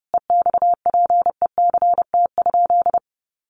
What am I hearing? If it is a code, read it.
Morse code, heard as a keyed tone, about 30 wpm, 700 Hz. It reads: EXPECT?
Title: Sentences from Top 1300 Words 30wpm